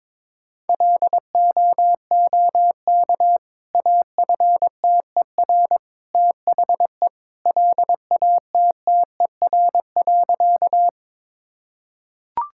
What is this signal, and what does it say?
Morse code, heard as a keyed tone, about 22 wpm, 700 Hz. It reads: LOOK AFTER THE LATTER.
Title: Sentences from Top 1600 Words 22wpm